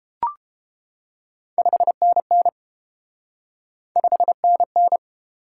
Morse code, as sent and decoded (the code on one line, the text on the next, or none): ..... -. -. / ..... -. -.
5NN 5NN